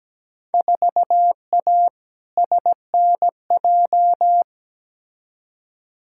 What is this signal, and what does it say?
Morse code, heard as a keyed tone, about 17 wpm, 700 Hz. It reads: 4A SNJ
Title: Field Day Exchanges Encore - 17wpm